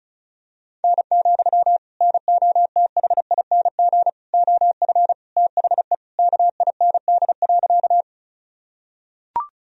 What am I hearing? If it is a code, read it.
Morse code, heard as a keyed tone, about 36 wpm, 700 Hz. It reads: N, NOTHING OF THE KIND.
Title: Sentences from Top 500 Words and Abbreviations 35wpm